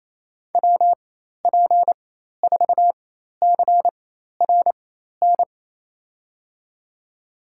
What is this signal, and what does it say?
Morse code, heard as a keyed tone, about 27 wpm, spaced as for 13 wpm, 700 Hz. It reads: WP4CRN
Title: USA Call Signs - 6 Characters - 4x Character Spacing - 28wpm